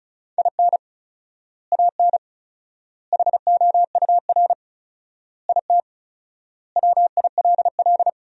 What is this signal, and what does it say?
Morse code, heard as a keyed tone, about 36 wpm, 700 Hz. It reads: IN AN HOUR IT WILL
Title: 4x Word-Spacing - Sentences from Top 400 Words - 35wpm